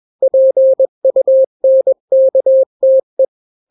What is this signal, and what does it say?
Morse code, heard as a keyed tone, about 21 wpm, 530 Hz. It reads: PUNKTE
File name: daily / 20wpm